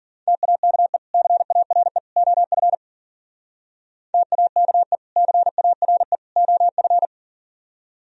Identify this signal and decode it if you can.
Morse code, heard as a keyed tone, about 52 wpm, 700 Hz. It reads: TAKE CARE OF TAKE CARE OF